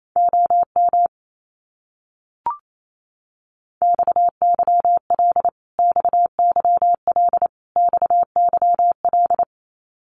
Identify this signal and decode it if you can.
Morse code, heard as a keyed tone, about 28 wpm, 700 Hz. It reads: OM XYL XYL XYL
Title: Abbreviations Repeated 3x - 28wpm